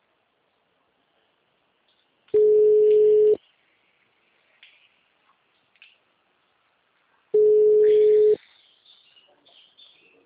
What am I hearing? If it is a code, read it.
Ringback tone (Europe, 425 Hz).